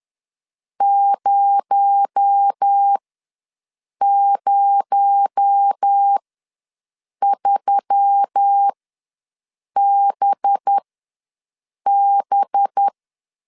Morse code, heard as a keyed tone, about 10 wpm, spaced as for 6 wpm, 800 Hz. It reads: 003BB
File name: LEZIONI / 6 PKX0 / 6 PKX0 ws 10wpm 4220